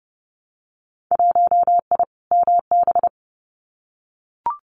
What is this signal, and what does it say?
Morse code, heard as a keyed tone, about 30 wpm, 700 Hz. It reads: 1I MB
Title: Winter Field Day Exchanges Encore 30wpm